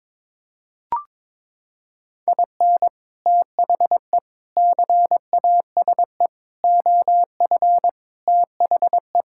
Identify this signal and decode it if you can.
Morse code, heard as a keyed tone, about 22 wpm, 700 Hz. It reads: IN THE CASE OF THE
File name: practice / 22wpm / Sentences with 4-Word Phrases - 22wpm